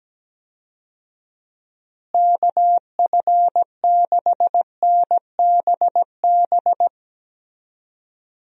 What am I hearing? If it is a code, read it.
Morse code, heard as a keyed tone, about 17 wpm, 700 Hz. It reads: KF6NBB
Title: USA Call Signs - 6 Characters - Repeated 3x - 17wpm